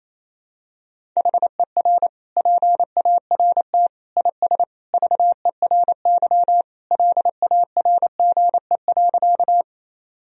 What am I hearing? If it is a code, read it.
Morse code, heard as a keyed tone, about 28 wpm, 700 Hz. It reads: HER PART IS VERY LARGE.
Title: Sentences from Top 1000 Words - Review Part 1 - 28wpm